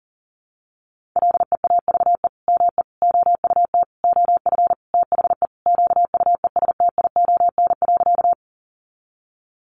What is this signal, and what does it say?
Morse code, heard as a keyed tone, about 40 wpm, 700 Hz. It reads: LEAVE ME OUT OF THE QUESTION.
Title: Sentences from Top 400 Words 40wpm